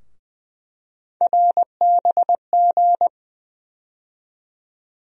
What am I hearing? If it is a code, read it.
Morse code, heard as a keyed tone, about 21 wpm, 700 Hz. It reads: RBG
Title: USA Call Sign Postfixes - 3 Letters 20wpm